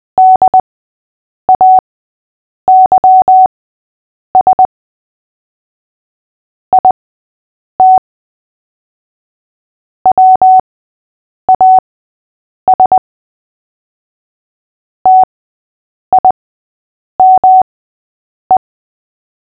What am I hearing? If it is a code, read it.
Morse code, heard as a keyed tone, about 20 wpm, spaced as for 8 wpm, 750 Hz. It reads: DAYS IT WAS TIME
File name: cw / fists / Short-Story-5-8wpm